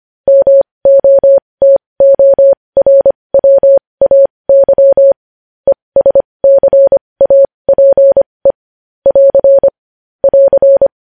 Morse code, heard as a keyed tone, about 25 wpm, spaced as for 20 wpm, 550 Hz. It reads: MOTORWAY ESCAPE <AR> <AR>
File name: podcast / News headlines-1x-25x20